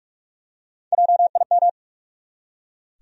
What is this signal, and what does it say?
Morse code, heard as a keyed tone, about 46 wpm, 700 Hz. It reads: JIM
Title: USA Common Names - Max 3 Letters - 45wpm